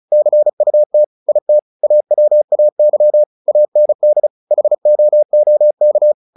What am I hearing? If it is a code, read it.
Morse code, heard as a keyed tone, about 34 wpm, 600 Hz. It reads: CUT IT AWAY AND HOOK
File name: the old man and the sea / the old man and the sea 1 35wpm